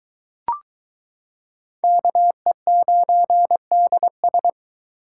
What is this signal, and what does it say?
Morse code, heard as a keyed tone, about 23 wpm, 700 Hz. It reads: KE9DS